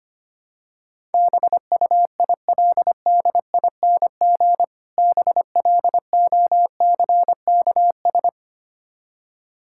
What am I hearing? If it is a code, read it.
Morse code, heard as a keyed tone, about 25 wpm, 700 Hz. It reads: BUILDING BLOCKS
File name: practice / 25wpm / Difficult Words with Hints - 25wpm